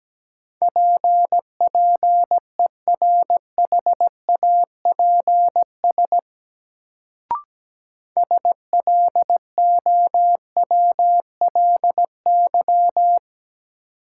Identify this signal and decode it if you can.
Morse code, heard as a keyed tone, about 17 wpm, 700 Hz. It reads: PPERHAPS SLOWLY
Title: Sentences from Top 1100 Words - New Word Drill - 17wpm